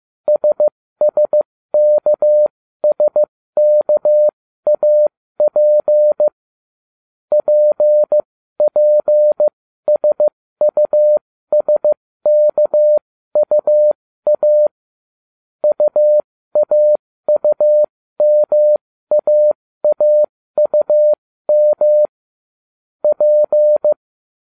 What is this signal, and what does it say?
Morse code, heard as a keyed tone, about 15 wpm, spaced as for 13 wpm, 600 Hz. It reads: SSKSKAP PPSUSKUA UAUMAAUM P